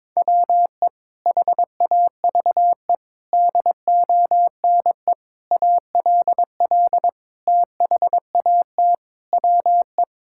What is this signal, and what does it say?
Morse code, heard as a keyed tone, about 21 wpm, 700 Hz. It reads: WE HAVE DONE ALL THAT WE
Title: Sentences from Top 1000 Words - Review Part 5 - 22wpm